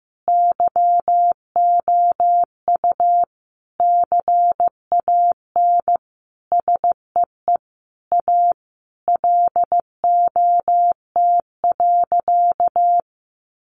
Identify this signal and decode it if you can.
Morse code, heard as a keyed tone, about 15 wpm, 700 Hz. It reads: YOU CAN SEE A LOT.
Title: Sentences from Top 1500 Words 15wpm